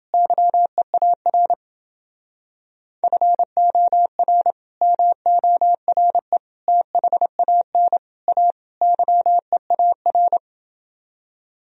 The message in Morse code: YEAR FOR MORE THAN A YEAR

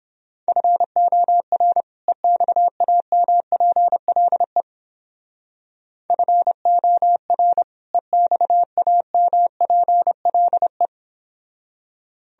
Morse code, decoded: FOR EXAMPLE FOR EXAMPLE

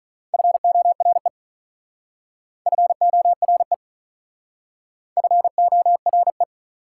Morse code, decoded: FORE FORE FORE

